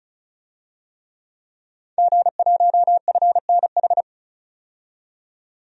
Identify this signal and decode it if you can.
Morse code, heard as a keyed tone, about 36 wpm, 700 Hz. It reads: G1FNH
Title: DXCC-100 Call Signs - 5 Characters - 35wpm